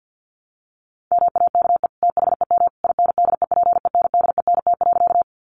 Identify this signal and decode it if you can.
Morse code, heard as a keyed tone, about 50 wpm, 700 Hz. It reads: MAKE THEM INDEPENDENT.
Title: Sentences from Top 900 Words 50wpm